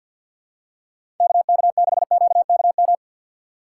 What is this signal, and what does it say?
Morse code, heard as a keyed tone, about 53 wpm, 700 Hz. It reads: KK6QKM
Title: USA Call Signs - 6 Characters - 3x Character Spacing - 50wpm